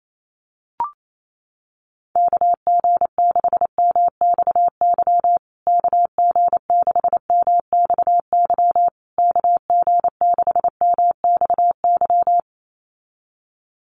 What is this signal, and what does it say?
Morse code, heard as a keyed tone, about 28 wpm, 690 Hz. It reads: KG6MXY KG6MXY KG6MXY
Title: USA Call Signs Repeated 3x 28wpm